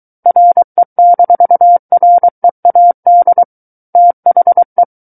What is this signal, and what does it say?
Morse code, heard as a keyed tone, about 23 wpm, 700 Hz. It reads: RE-READ THE
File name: Book One / 14-Chapter 14 - 23 WPM